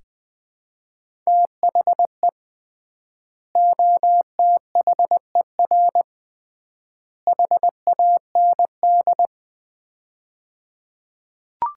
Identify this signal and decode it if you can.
Morse code, heard as a keyed tone, about 20 wpm, 700 Hz. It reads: THE OTHER HAND
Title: 3x Word-Spacing - Sets of 3 Words - 20wpm